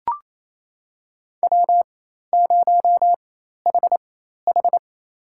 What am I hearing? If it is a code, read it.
Morse code, heard as a keyed tone, about 28 wpm, spaced as for 13 wpm, 700 Hz. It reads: W0HH
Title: USA Call Signs - 4 Characters - 4x Character Spacing - 28wpm